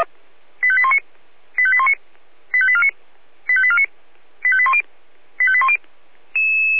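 Five-tone selective call (ZVEI), heard as quick runs of five tones, tones of about 70 ms. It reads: 85819 85819 85849 85849 85810 85810
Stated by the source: ZVEI five-tone selective call 85874 received by SDR.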